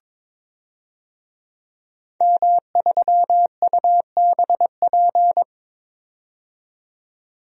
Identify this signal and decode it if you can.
Morse code, heard as a keyed tone, about 21 wpm, 700 Hz. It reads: M3UBP
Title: DXCC-100 Call Signs - 5 Characters - 22wpm